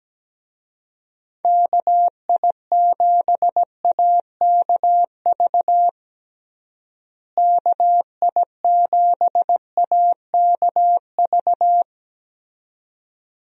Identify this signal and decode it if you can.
Morse code, heard as a keyed tone, about 17 wpm, 700 Hz. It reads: KI7AKV KI7AKV
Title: USA Call Signs - 6 Characters - Repeated 2x - 17wpm